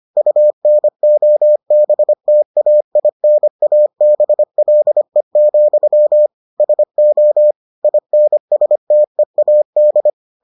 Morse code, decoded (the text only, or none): UNOBTAINABLE, SO INSTEAD